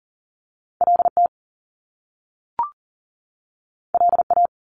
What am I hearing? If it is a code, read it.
Morse code, heard as a keyed tone, about 42 wpm, 700 Hz. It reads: LT LA
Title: Top Two Letter Combinations in Words 40wpm